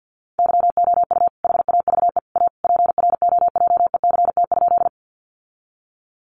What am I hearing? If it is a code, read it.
Morse code, heard as a keyed tone, about 50 wpm, 700 Hz. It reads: YOU HAVE A PROJECT?
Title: Sentences from Top 1400 Words 50wpm